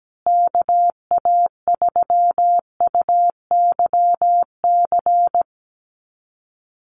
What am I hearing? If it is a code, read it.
Morse code, heard as a keyed tone, about 17 wpm, 700 Hz. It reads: KA3UYC
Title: USA Call Signs Repeated 2x 17wpm